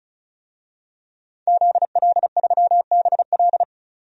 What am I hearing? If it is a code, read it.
Morse code, heard as a keyed tone, about 36 wpm, 700 Hz. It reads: ZL3BL
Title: DXCC-100 Call Signs - 5 Characters - 35wpm